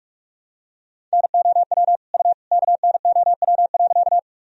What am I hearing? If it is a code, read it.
Morse code, heard as a keyed tone, about 46 wpm, 700 Hz. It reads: NOW U KNOW.
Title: Sentences from Top 100 Words and Abbreviations 45wpm